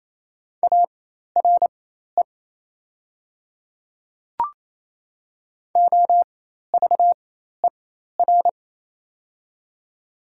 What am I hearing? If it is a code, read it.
Morse code, heard as a keyed tone, about 28 wpm, spaced as for 13 wpm, 700 Hz. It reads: ARE OVER
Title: 4x Character Spacing - Top 100 Words - 28wpm